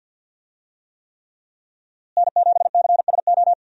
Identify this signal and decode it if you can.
Morse code, heard as a keyed tone, about 48 wpm, 700 Hz. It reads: N7CSO
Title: USA Call Signs - 5 Characters - Repeated 3x - 50wpm